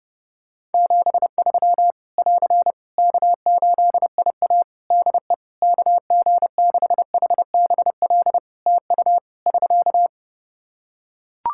Morse code, heard as a keyed tone, about 32 wpm, 700 Hz. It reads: 73 <AR> K8IA DE KG6HBL TU <SK>
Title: Practice QSO - Exchange 4 - Ending - 30wpm